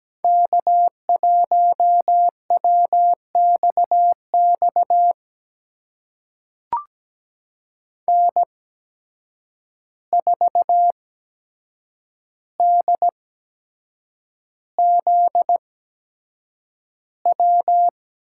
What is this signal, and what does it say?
Morse code, heard as a keyed tone, about 17 wpm, spaced as for 7 wpm, 700 Hz. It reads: K1WXX N4DZW
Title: USA Call Signs - 5 Characters - 8x Character Spacing - 17wpm